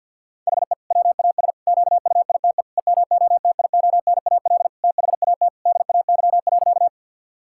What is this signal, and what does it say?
Morse code, heard as a keyed tone, about 49 wpm, 700 Hz. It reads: HE WAS QUITE EMOTIONAL THAT DAY.
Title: Sentences from Top 1000 Words - Review Part 5 - 50wpm